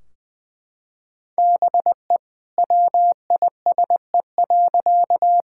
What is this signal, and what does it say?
Morse code, heard as a keyed tone, about 20 wpm, 700 Hz. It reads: BE WISE.
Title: Sentences from Top 1600 Words 20wpm